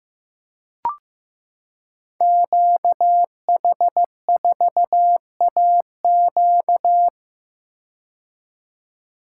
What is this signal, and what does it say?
Morse code, heard as a keyed tone, about 15 wpm, 700 Hz. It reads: QH4AQ
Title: Code Groups - 5 Letter:Numbers - 15wpm